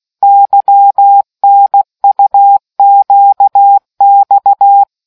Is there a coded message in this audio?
Yes: Morse code from 0.2 to 4.8 s, heard as a keyed tone, about 16 wpm, 800 Hz. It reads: YNUQX